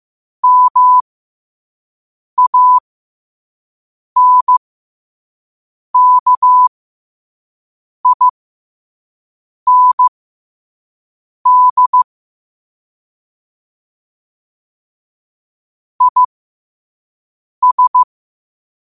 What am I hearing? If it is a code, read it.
Morse code, heard as a keyed tone, about 15 wpm, spaced as for 5 wpm, 1000 Hz. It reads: MANKIND IS
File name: 5wpm